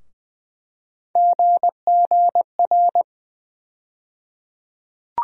Morse code, heard as a keyed tone, about 21 wpm, 700 Hz. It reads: GGR
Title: USA Call Sign Postfixes - 3 Letters 20wpm